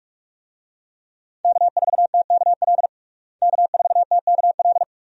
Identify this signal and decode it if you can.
Morse code, heard as a keyed tone, about 46 wpm, 700 Hz. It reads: K4TKL K4TKL
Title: USA Call Signs - 5 Characters - Repeated 2x - 45wpm